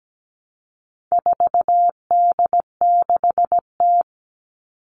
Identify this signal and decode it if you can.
Morse code, heard as a keyed tone, about 17 wpm, 700 Hz. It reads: 4D6T
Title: All Cut Numbers - 4-Digit 17wpm